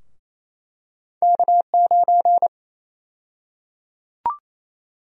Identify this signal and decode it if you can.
Morse code, heard as a keyed tone, about 28 wpm, 700 Hz. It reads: K9
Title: USA Call Sign Prefixes - Repeated 2x - 28wpm